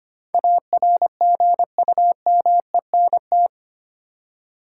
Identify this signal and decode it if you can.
Morse code, heard as a keyed tone, about 25 wpm, 700 Hz. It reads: ARGUMENT
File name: Eight Letter Words - 25wpm